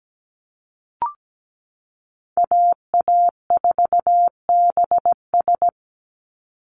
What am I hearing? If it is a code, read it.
Morse code, heard as a keyed tone, about 17 wpm, 700 Hz. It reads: AA4BS
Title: USA Call Signs 17wpm